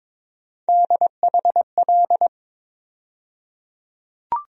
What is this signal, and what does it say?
Morse code, heard as a keyed tone, about 22 wpm, 700 Hz. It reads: DHL
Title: USA Call Sign Postfixes - 3 Letters 22wpm